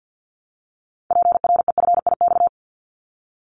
Morse code, heard as a keyed tone, about 50 wpm, 700 Hz. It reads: PREFIX
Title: Callsign Prefixes for Top 100 DXCC - 50wpm